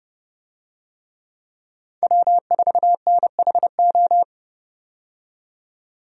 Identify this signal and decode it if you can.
Morse code, heard as a keyed tone, about 30 wpm, 700 Hz. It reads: W4NHO
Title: USA Call Signs - 5 Characters - 30wpm